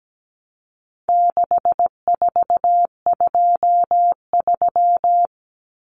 Morse code, heard as a keyed tone, about 17 wpm, 700 Hz. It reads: 6423
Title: Numbers - 4-Digit 17wpm